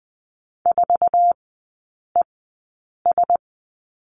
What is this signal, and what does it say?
Morse code, heard as a keyed tone, about 20 wpm, spaced as for 8 wpm, 700 Hz. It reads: 4ES
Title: 087 Lesson 15e - Number 5 - Three Characters 20wpm